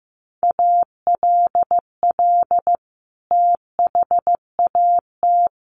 Morse code, heard as a keyed tone, about 15 wpm, 700 Hz. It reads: ALL THAT
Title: Sentences from Top 500 Words 15wpm